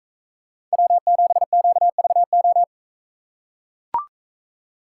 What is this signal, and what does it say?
Morse code, heard as a keyed tone, about 41 wpm, 700 Hz. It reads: W7QVO